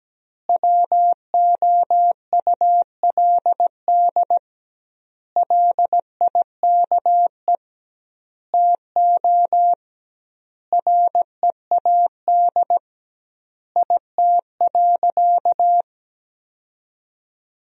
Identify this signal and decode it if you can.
Morse code, heard as a keyed tone, about 17 wpm, 700 Hz. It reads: WOULD LIKE TO READ IT.
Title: 2x Word-Spacing - Sentences from Top 500 Words - 17wpm